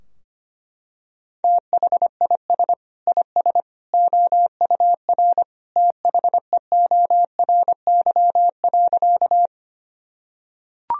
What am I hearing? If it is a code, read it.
Morse code, heard as a keyed tone, about 25 wpm, 700 Hz. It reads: THIS IS OUR THEORY.
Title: Sentences from Top 1000 Words - Review Part 3 - 25wpm